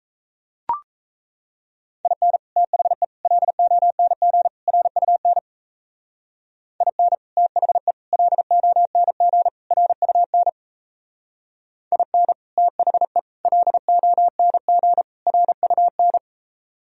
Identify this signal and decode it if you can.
Morse code, heard as a keyed tone, about 35 wpm, 700 Hz. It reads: IN THE LONG RUN IN THE LONG RUN IN THE LONG RUN